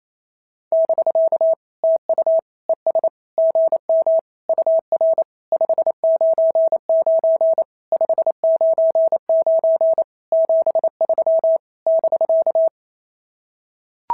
Morse code, heard as a keyed tone, about 28 wpm, 650 Hz. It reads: <BK> TU ES GM UR 599 599 73 <BK>
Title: SOTA QSO Practice 28wpm